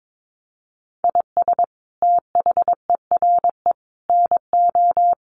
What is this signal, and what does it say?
Morse code, heard as a keyed tone, about 22 wpm, 700 Hz. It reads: IS THERE NO
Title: Sentences from Top 200 Words 22wpm